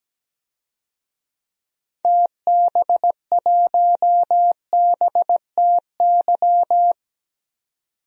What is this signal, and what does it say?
Morse code, heard as a keyed tone, about 17 wpm, 700 Hz. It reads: TB1BTY
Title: DXCC-100 Call Signs - 6 Characters - 17wpm